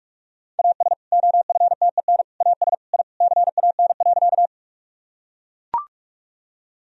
Morse code, heard as a keyed tone, about 42 wpm, 700 Hz. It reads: AS OFTEN AS I CAN.
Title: Sentences from Top 1000 Words - Review Part 1 - 45wpm